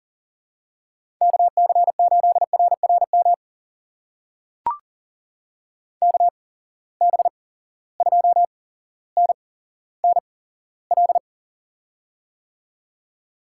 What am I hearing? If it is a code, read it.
Morse code, heard as a keyed tone, about 41 wpm, spaced as for 17 wpm, 700 Hz. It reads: KC8RRM KB2NNL
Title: USA Call Signs - 6 Characters - 8x Character Spacing - 40wpm